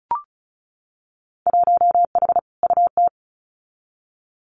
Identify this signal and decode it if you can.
Morse code, heard as a keyed tone, about 36 wpm, 700 Hz. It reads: 1H UT
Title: Winter Field Day Exchanges 35wpm